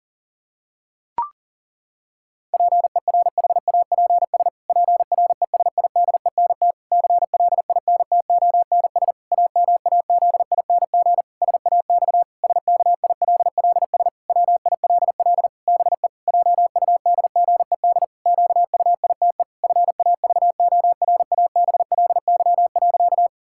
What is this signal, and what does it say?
Morse code, heard as a keyed tone, about 37 wpm, 700 Hz. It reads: PERHAPS PRESIDENT CLINTONS AMAZING SAX SKILLS WILL BE JUDGED QUITE FAVORABLY.
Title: Pangram Challenge v2 - 40wpm